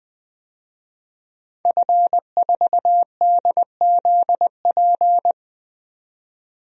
Morse code, heard as a keyed tone, about 20 wpm, 700 Hz. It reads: F4DZP
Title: DXCC-100 Call Signs - 5 Characters - 20wpm